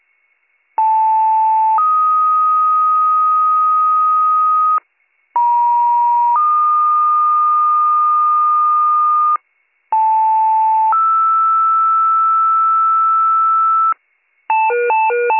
Two-tone page, heard as a one-second tone, then a three-second tone, about 880 then 1250 Hz, about 950 then 1220 Hz, about 880 then 1320 Hz. It is heard from 0.8 to 13.9 s.